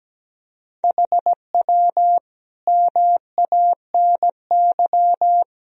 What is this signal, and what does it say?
Morse code, heard as a keyed tone, about 17 wpm, 700 Hz. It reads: HW MANY
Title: Sentences from Top 400 Words and Abbreviations 17wpm